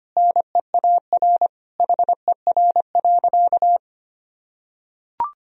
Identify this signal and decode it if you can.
Morse code, heard as a keyed tone, about 25 wpm, 700 Hz. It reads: NEAR HER.
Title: Sentences from Top 1000 Words - Review Part 4 - 25wpm